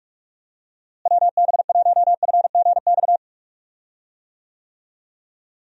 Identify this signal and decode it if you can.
Morse code, heard as a keyed tone, about 43 wpm, 700 Hz. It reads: WB1FGX
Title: USA Call Signs - 6 Characters - Repeated 3x - 45wpm